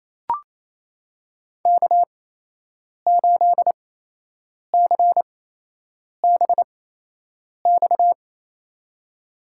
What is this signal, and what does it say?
Morse code, heard as a keyed tone, about 28 wpm, spaced as for 8 wpm, 700 Hz. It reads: K8CBX
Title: USA Call Signs - 8x Character Spacing - 28wpm